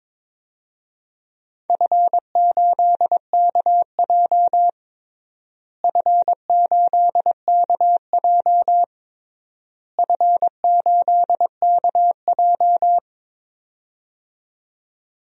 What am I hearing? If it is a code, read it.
Morse code, heard as a keyed tone, about 22 wpm, 700 Hz. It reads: F8KJ F8KJ F8KJ
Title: DXCC-100 Call Signs - 4 Characters - Repeated 3x - 22wpm